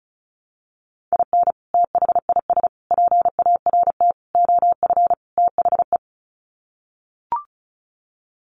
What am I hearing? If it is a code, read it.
Morse code, heard as a keyed tone, about 34 wpm, 700 Hz. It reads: IN THIS PART OF THE
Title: Sets of 5 Words 35wpm